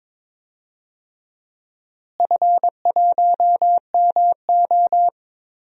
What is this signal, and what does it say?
Morse code, heard as a keyed tone, about 21 wpm, 700 Hz. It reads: F1MO
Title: DXCC-100 Call Signs - 4 Characters - Repeated 3x - 22wpm